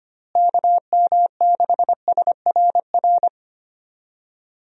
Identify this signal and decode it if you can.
Morse code, heard as a keyed tone, about 25 wpm, 700 Hz. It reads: KM6SRR
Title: USA Call Signs - 6 Characters - Repeated 3x - 25wpm